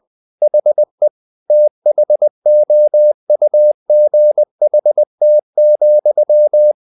Morse code, heard as a keyed tone, about 20 wpm, 600 Hz. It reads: HE THOUGHT,
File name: the old man and the sea part 2 20wpm